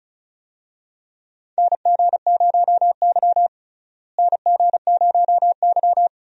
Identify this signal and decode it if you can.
Morse code, heard as a keyed tone, about 36 wpm, 700 Hz. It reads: NG0Y NG0Y
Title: USA Call Signs - 4 Characters - Repeated 2x - 35wpm